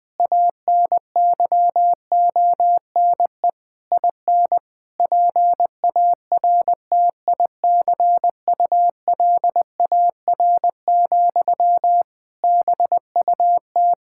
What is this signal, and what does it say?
Morse code, heard as a keyed tone, about 20 wpm, 700 Hz. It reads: ANYONE IN PARTICULAR, BUT
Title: Sentences from Top 1000 Words - Review Part 5 - 20wpm